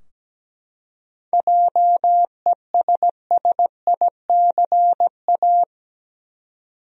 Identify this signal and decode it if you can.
Morse code, heard as a keyed tone, about 17 wpm, 700 Hz. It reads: JESSICA
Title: USA Common Names - 17wpm